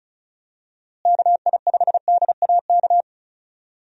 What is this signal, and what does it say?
Morse code, heard as a keyed tone, about 36 wpm, 700 Hz. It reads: KI5DAK
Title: USA Call Signs - 6 Characters - Repeated 3x - 35wpm